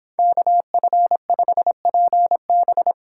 Morse code, heard as a keyed tone, about 26 wpm, 700 Hz. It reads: KF5PB